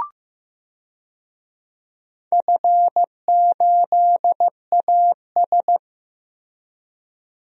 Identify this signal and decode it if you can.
Morse code, heard as a keyed tone, about 15 wpm, 700 Hz. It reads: F8AS
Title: DXCC-100 Call Signs - 4 Characters - Repeated 3x - 15wpm